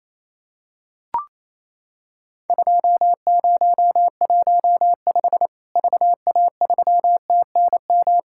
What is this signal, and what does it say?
Morse code, heard as a keyed tone, about 28 wpm, 700 Hz. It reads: 2015 VA3TNM
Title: Worked All Europe DX Contest - QTC Exchange - 28wpm